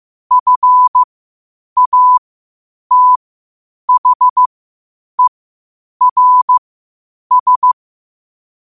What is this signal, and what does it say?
Morse code, heard as a keyed tone, about 15 wpm, spaced as for 8 wpm, 1000 Hz. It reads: FATHERS
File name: qotdfiles / 8wpm